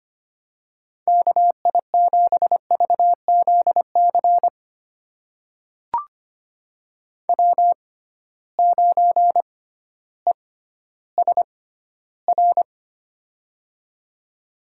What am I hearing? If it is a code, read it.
Morse code, heard as a keyed tone, about 25 wpm, spaced as for 14 wpm, 700 Hz. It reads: KI7VZC W9ESR
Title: USA Call Signs - 6x Character Spacing - 25wpm